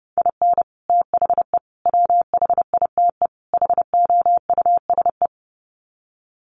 Morse code, heard as a keyed tone, about 30 wpm, 700 Hz. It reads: IN THE WHITE HOUSE
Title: Sets of 4 Words 30wpm